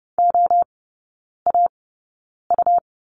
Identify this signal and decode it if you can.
Morse code, heard as a keyed tone, about 30 wpm, spaced as for 9 wpm, 700 Hz. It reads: OAU
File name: Lesson 17e - Letter u - Three Characters 30wpm